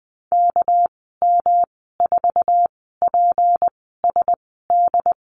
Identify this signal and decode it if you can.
Morse code, heard as a keyed tone, about 19 wpm, spaced as for 14 wpm, 700 Hz. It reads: KM4PSD
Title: 213 Lesson 34g - Number 8 - Call Signs 20wpm